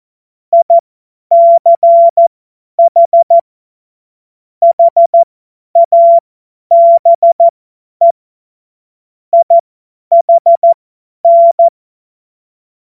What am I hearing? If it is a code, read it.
Morse code, heard as a keyed tone, about 14 wpm, spaced as for 10 wpm, 680 Hz. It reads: ICH HABE IHN
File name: Morsekurs / Witze / Witz 18. Mai 2021, 50 F70